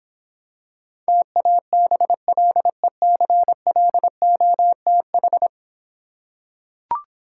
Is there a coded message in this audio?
Yes: Morse code from 1.1 to 5.5 s, heard as a keyed tone, about 26 wpm, 700 Hz. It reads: TABLECLOTH